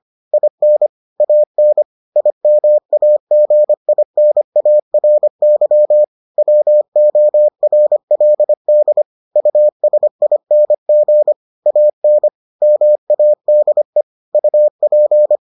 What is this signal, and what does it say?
Morse code, heard as a keyed tone, about 25 wpm, 600 Hz. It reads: IN AN IMAGINARY WORLD USING AN MADE UP